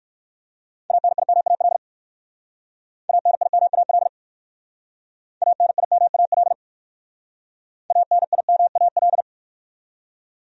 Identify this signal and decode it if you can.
Morse code, heard as a keyed tone, about 56 wpm, 700 Hz. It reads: ANIMAL ANIMAL ANIMAL ANIMAL
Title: Speed-Racing Top 500 Words 45wpm